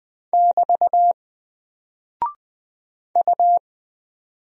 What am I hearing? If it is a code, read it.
Morse code, heard as a keyed tone, about 20 wpm, 700 Hz. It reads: <BT> U
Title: Single Character (Letter:Number:Punctuation:Prosign) Alt 20wpm